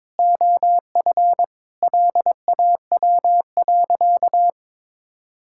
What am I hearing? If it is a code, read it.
Morse code, heard as a keyed tone, about 22 wpm, 700 Hz. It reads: OF LAW.
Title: Sentences from Top 1000 Words - Review Part 5 - 22wpm